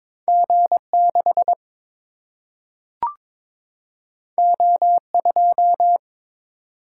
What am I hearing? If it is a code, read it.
Morse code, heard as a keyed tone, about 22 wpm, 700 Hz. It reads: G 6 O 2
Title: Single-Letter Single-Number 22wpm